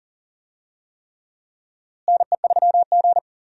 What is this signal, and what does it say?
Morse code, heard as a keyed tone, about 39 wpm, 700 Hz. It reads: NE3G
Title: USA Call Signs - 4 Characters - Repeated 3x - 40wpm